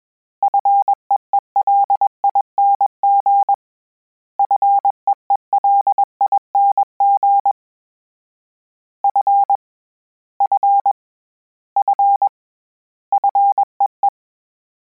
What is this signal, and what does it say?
Morse code, heard as a keyed tone, about 21 wpm, 800 Hz. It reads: FEELING FEELING F F F FEE
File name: Combinations / Combinations-mp3s / Combinations@20